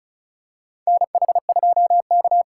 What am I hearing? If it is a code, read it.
Morse code, heard as a keyed tone, about 34 wpm, 700 Hz. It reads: NH2K
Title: USA Call Signs - 2x Character Spacing - 35wpm